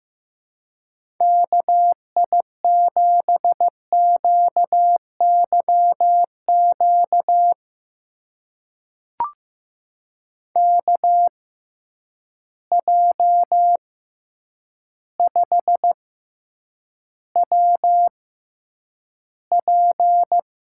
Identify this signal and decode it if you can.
Morse code, heard as a keyed tone, about 15 wpm, spaced as for 8 wpm, 700 Hz. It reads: KI7QYQ KJ5WP
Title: USA Call Signs - 6x Character Spacing - 15wpm